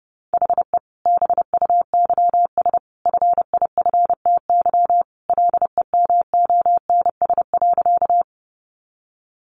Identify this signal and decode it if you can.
Morse code, heard as a keyed tone, about 30 wpm, 700 Hz. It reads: HE BUYS FIFTY LEMONS.
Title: Additional Easy Phrases 30wpm